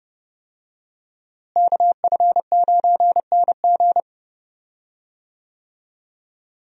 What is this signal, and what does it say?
Morse code, heard as a keyed tone, about 30 wpm, 700 Hz. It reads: KF9NG
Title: USA Call Signs - 5 Characters - Repeated 3x - 30wpm